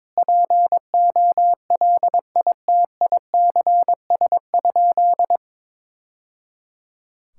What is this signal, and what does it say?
Morse code, heard as a keyed tone, about 22 wpm, 700 Hz. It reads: POLITICS?
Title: Sentences from Top 1000 Words - Review Part 6 - 22wpm